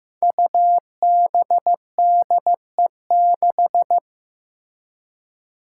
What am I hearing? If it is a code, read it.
Morse code, heard as a keyed tone, about 15 wpm, 700 Hz. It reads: UBDE6
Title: Five Challenging Letter-Number Combinations 15wpm